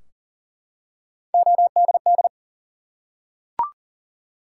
Morse code, heard as a keyed tone, about 38 wpm, 700 Hz. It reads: ODD
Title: Three Letter Words - 40wpm